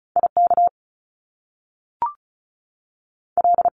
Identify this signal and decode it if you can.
Morse code, heard as a keyed tone, about 36 wpm, 700 Hz. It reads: I K L
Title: Top Two Letter Combinations in Words 35wpm